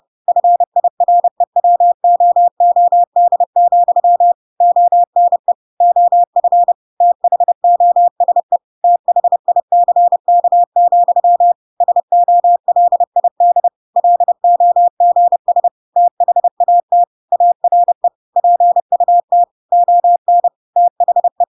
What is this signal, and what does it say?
Morse code, heard as a keyed tone, about 30 wpm, 700 Hz. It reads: FIREWOOD, ONE OF THOSE THICK, SOLID LOGS THAT ARE PUT ON THE
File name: Pinocchio / Pinocchio ch1 30 30 ik1hge